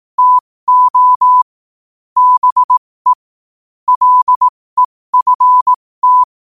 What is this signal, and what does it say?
Morse code, heard as a keyed tone, about 18 wpm, spaced as for 15 wpm, 1000 Hz. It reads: TO BE LEFT
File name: qotdfiles / 15wpm